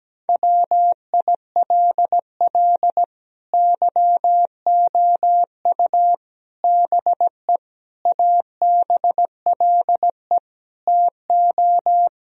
Morse code, heard as a keyed tone, about 17 wpm, 700 Hz. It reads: WILL YOU BE ABLE TO
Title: Sentences from Top 1000 Words - Review Part 2 - 17wpm